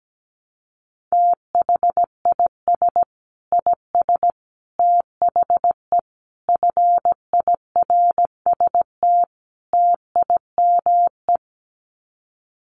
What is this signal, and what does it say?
Morse code, heard as a keyed tone, about 17 wpm, 700 Hz. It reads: THIS IS THE FIRST TIME
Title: Sets of 5 Words 17wpm